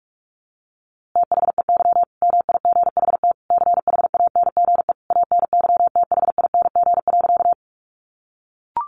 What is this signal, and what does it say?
Morse code, heard as a keyed tone, about 46 wpm, 700 Hz. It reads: THEY MIGHT CHANGE ANYTHING.
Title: A Sound Of Thunder 45wpm